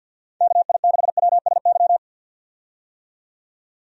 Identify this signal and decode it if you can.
Morse code, heard as a keyed tone, about 49 wpm, 700 Hz. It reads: KI6WSY
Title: USA Call Signs - 6 Characters - 50wpm